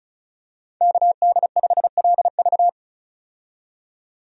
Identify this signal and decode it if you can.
Morse code, heard as a keyed tone, about 36 wpm, 700 Hz. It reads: KD5LV
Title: USA Call Signs - 5 Characters - 35wpm